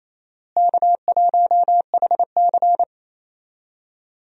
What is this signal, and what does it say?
Morse code, heard as a keyed tone, about 28 wpm, 700 Hz. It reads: K1HC
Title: USA Call Signs - 4 Characters - Repeated 2x - 28wpm